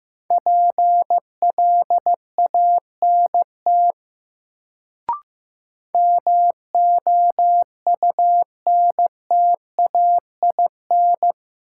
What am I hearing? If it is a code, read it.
Morse code, heard as a keyed tone, about 15 wpm, 700 Hz. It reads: PLANT MOUNTAIN
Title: Sentences from Top 1000 Words - Review Part 1 - New Word Drill - 15wpm